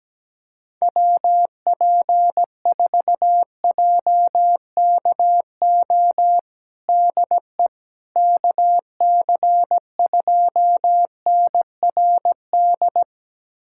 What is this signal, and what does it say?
Morse code, heard as a keyed tone, about 17 wpm, 700 Hz. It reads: WP4JKO DE KC2NRD
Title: USA Callsign DE Callsign 17wpm